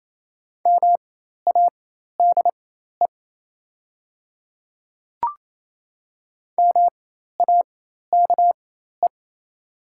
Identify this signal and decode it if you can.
Morse code, heard as a keyed tone, about 28 wpm, spaced as for 13 wpm, 700 Hz. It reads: MADE MAKE
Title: 4x Character Spacing - Top 200 Words - 28wpm